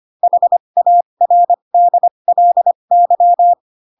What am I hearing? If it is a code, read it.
Morse code, heard as a keyed tone, about 25 wpm, spaced as for 22 wpm, 700 Hz. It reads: HARDLY